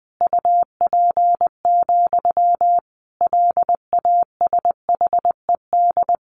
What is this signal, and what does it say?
Morse code, heard as a keyed tone, about 20 wpm, 700 Hz. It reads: UP, LASHED
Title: A Sound Of Thunder 20wpm